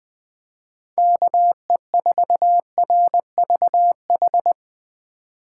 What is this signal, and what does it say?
Morse code, heard as a keyed tone, about 23 wpm, 700 Hz. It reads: KE4RVH
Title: Speed Racing - USA 2x3 Call Signs 20wpm